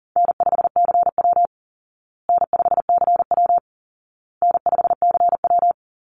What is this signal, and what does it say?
Morse code, heard as a keyed tone, about 37 wpm, 700 Hz. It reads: N5CW N5CW N5CW
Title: Easy Call Signs Repeated 3x - 40wpm